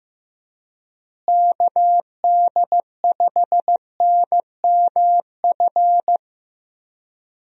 Morse code, heard as a keyed tone, about 15 wpm, 700 Hz. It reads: KD5NMF
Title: USA Call Signs - 6 Characters - Repeated 3x - 15wpm